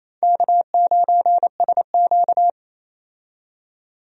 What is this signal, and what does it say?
Morse code, heard as a keyed tone, about 28 wpm, 700 Hz. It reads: K9SQ
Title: USA Call Signs - 4 Characters - 28wpm